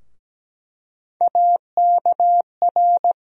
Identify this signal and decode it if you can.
Morse code, heard as a keyed tone, about 17 wpm, 700 Hz. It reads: AKR
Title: USA Call Sign Postfixes - 3 Letters 17wpm